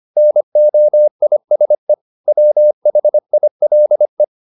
Morse code, heard as a keyed tone, about 25 wpm, 600 Hz. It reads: NOISE WHILE